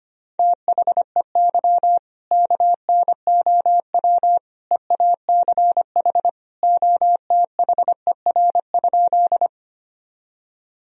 Morse code, heard as a keyed tone, about 25 wpm, 700 Hz. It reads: THEY KNOW EACH OTHER?
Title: Sentences from Top 1000 Words - Review Part 1 - 25wpm